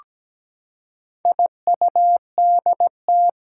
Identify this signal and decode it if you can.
Morse code, heard as a keyed tone, about 17 wpm, 700 Hz. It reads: IUDT
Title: Four Challenging Letter-Number Combinations 17wpm